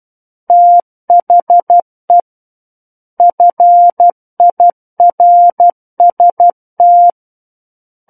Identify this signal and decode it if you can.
Morse code, heard as a keyed tone, about 12 wpm, 700 Hz. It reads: THE FIRST